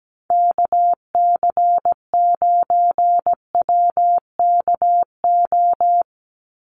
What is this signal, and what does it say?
Morse code, heard as a keyed tone, about 17 wpm, 700 Hz. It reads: KC9WKO
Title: USA Call Signs 17wpm